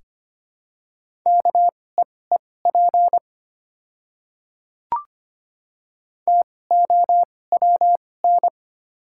Morse code, heard as a keyed tone, about 25 wpm, spaced as for 18 wpm, 700 Hz. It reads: KEEP TOWN
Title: 2x Character Spacing - Top 500 Words - 25wpm